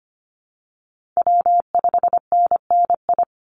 Morse code, heard as a keyed tone, about 25 wpm, 700 Hz. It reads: W5NNI
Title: USA Call Signs 25wpm